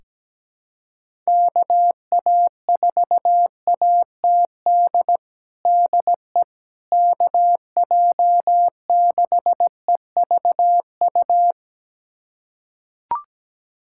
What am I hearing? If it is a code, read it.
Morse code, heard as a keyed tone, about 17 wpm, 700 Hz. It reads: KA4ATD DE KJ6EVU
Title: USA Callsign DE Callsign - 6 Characters - 17wpm